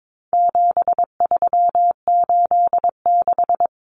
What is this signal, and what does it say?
Morse code, heard as a keyed tone, about 22 wpm, 700 Hz. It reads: 7386
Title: Common Cut Numbers - 4-Digit 22wpm